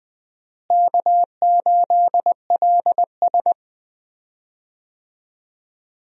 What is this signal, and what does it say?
Morse code, heard as a keyed tone, about 20 wpm, 700 Hz. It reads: K8LS
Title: USA Call Signs - 4 Characters - Repeated 2x - 20wpm